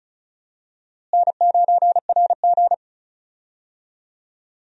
Morse code, heard as a keyed tone, about 34 wpm, 700 Hz. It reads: N9RG
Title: USA Call Signs - 4 Characters - Repeated 3x - 35wpm